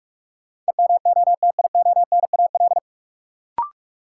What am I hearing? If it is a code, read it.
Morse code, heard as a keyed tone, about 46 wpm, 700 Hz. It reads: EMOTIONAL
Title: Sentences from Top 1000 Words - Review Part 5 - New Word Drill - 45wpm